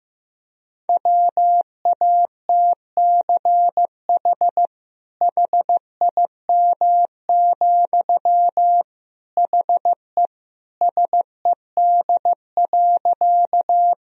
Morse code, heard as a keyed tone, about 15 wpm, 700 Hz. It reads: WATCH HIM, HE SED.
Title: Sentences from Top 300 Words and Abbreviations 15wpm